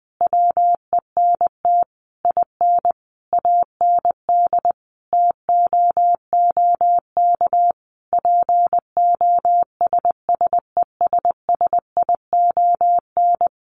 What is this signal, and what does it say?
Morse code, heard as a keyed tone, about 19 wpm, 700 Hz. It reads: WENT IN AND TOOK POSSESSION
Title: Sentences from Top 1400 Words 20wpm